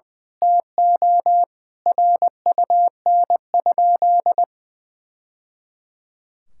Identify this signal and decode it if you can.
Morse code, heard as a keyed tone, about 21 wpm, 700 Hz. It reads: TO RUN?
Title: Sentences from Top 1000 Words - Review Part 6 - 20wpm